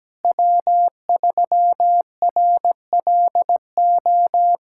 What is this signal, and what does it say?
Morse code, heard as a keyed tone, about 17 wpm, 700 Hz. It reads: W3RLO
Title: K1USN SST QSO Practice - 17wpm